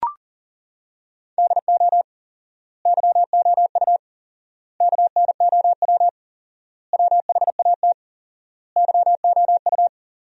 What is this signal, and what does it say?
Morse code, heard as a keyed tone, about 37 wpm, 700 Hz. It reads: DO YOU KNOW WHAT YOU
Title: 4x Word-Spacing - Sentences from Top 100 Words - 40wpm